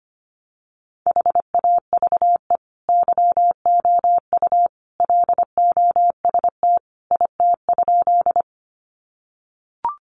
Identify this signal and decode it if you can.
Morse code, heard as a keyed tone, about 25 wpm, 700 Hz. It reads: HAVE YOU LOST IT?
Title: Sentences from Top 1400 Words 25wpm